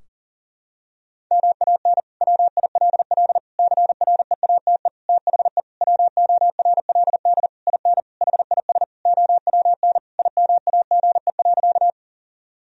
Morse code, heard as a keyed tone, about 40 wpm, 700 Hz. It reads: MAN WILL CREATE THE WORLD IN HIS OWN IMAGE.
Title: Sentences from Top 1000 Words - Review Part 4 - 40wpm